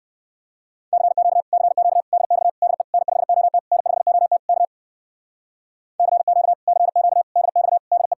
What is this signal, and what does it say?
Morse code, heard as a keyed tone, about 65 wpm, 700 Hz. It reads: CQ CQ DX DE N5OT N5OT K CQ CQ DX DE